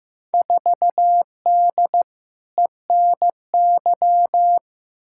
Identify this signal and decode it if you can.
Morse code, heard as a keyed tone, about 15 wpm, 700 Hz. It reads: 4D ENY
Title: Field Day Exchanges Encore - 15wpm